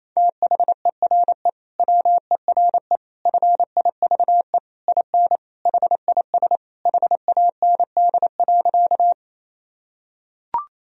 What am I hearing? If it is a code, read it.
Morse code, heard as a keyed tone, about 27 wpm, 700 Hz. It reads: THERE WERE FIVE IN HIS HAND.
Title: Sentences from Top 1000 Words - Review Part 2 - 28wpm